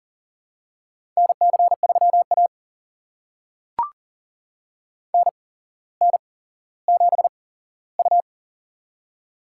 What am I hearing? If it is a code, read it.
Morse code, heard as a keyed tone, about 40 wpm, spaced as for 17 wpm, 700 Hz. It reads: NC3A NN7U
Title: USA Call Signs - 4 Characters - 8x Character Spacing - 40wpm